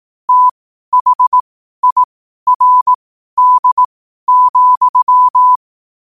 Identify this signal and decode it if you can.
Morse code, heard as a keyed tone, about 18 wpm, spaced as for 12 wpm, 1000 Hz. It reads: THIRD,
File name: qotdfiles / 12wpm